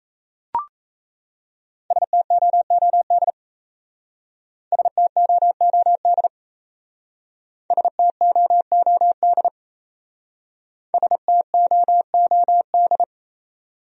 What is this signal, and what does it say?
Morse code, heard as a keyed tone, about 35 wpm, 700 Hz. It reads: STOOD STOOD STOOD STOOD
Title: Speed-Racing Top 500 Words 28wpm